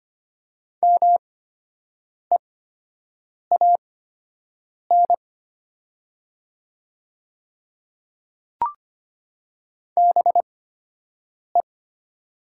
Morse code, heard as a keyed tone, about 25 wpm, spaced as for 7 wpm, 700 Hz. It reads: MEAN BE
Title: 8x Character Spacing - Top 200 Words - 25wpm